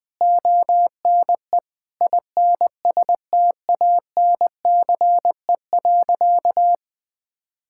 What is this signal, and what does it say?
Morse code, heard as a keyed tone, about 21 wpm, 700 Hz. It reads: ONE INSTANCE.
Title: Sentences from Top 1000 Words - Review Part 5 - 20wpm